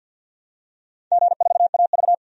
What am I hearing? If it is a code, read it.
Morse code, heard as a keyed tone, about 50 wpm, 700 Hz. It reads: G4AV
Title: DXCC-100 Call Signs - 4 Characters - 50wpm